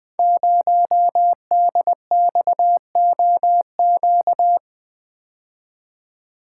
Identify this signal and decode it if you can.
Morse code, heard as a keyed tone, about 20 wpm, 700 Hz. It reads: 0DXOQ
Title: Code Groups - 5 Letter:Numbers - 20wpm